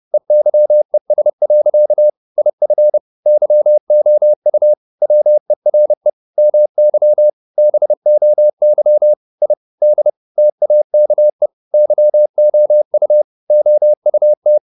Morse code, heard as a keyed tone, about 30 wpm, 600 Hz. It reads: EYES. IF YOU WERE MY BOY I D TAKE YOU OUT
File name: the old man and the sea / the old man and the sea 1 30wpm